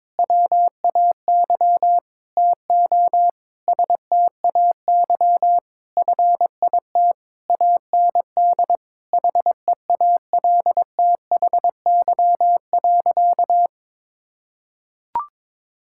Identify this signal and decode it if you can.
Morse code, heard as a keyed tone, about 22 wpm, 700 Hz. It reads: WAY TO STAY FIT AND HEALTHY.
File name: Difficult Sentences with Hints - 22wpm